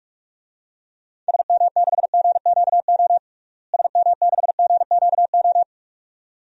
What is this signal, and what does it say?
Morse code, heard as a keyed tone, about 46 wpm, 700 Hz. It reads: SM6GQO SM6GQO
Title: DXCC-100 Call Signs - 6 Characters - Repeated 2x - 45wpm